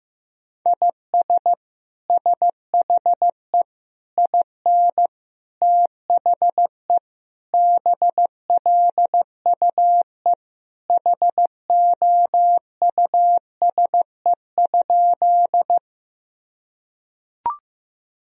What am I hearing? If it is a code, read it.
Morse code, heard as a keyed tone, about 15 wpm, 700 Hz. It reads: IS SHE IN THE BLUE HOUSE?
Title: Sentences from Top 1000 Words - Review Part 3 - 15wpm